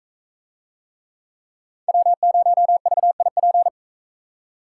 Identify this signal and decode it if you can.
Morse code, heard as a keyed tone, about 42 wpm, 700 Hz. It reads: W0VIP